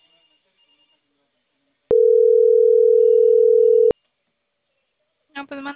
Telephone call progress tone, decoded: ringback tone

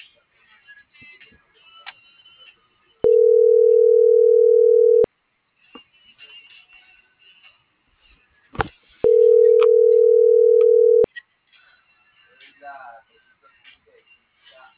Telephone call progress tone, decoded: ringback tone

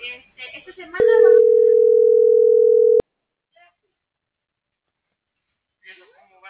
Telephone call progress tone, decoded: ringback tone

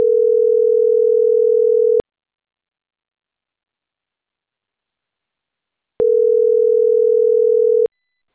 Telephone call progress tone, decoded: ringback tone

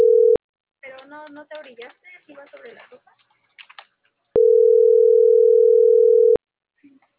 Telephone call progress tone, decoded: ringback tone

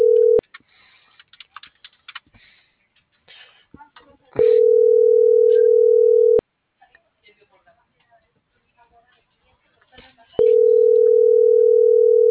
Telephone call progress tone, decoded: ringback tone

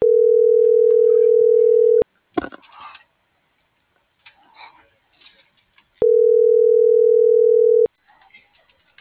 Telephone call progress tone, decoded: ringback tone

